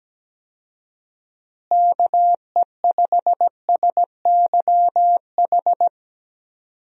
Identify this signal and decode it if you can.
Morse code, heard as a keyed tone, about 17 wpm, 700 Hz. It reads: KE5SYH
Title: USA Call Signs - 6 Characters - Repeated 2x - 17wpm